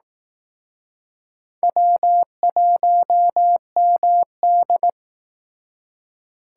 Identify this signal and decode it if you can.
Morse code, heard as a keyed tone, about 18 wpm, 700 Hz. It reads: W1MD
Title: Speed Racing - USA 1x2 Call Signs 15wpm